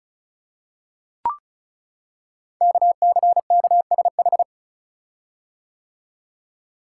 Morse code, heard as a keyed tone, about 36 wpm, 690 Hz. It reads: KCKSH
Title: Code Groups - 5 Letters - 35wpm